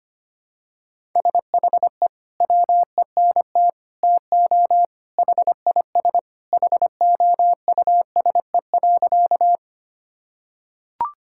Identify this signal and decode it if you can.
Morse code, heard as a keyed tone, about 25 wpm, 700 Hz. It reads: SHE WENT TO HIS HOUSE.
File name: practice / 25wpm / Sentences with 2-Word Phrases - 25wpm